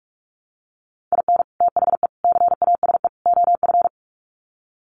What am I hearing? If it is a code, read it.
Morse code, heard as a keyed tone, about 44 wpm, 700 Hz. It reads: IN THE CASE OF
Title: Sets of 4 Words 45wpm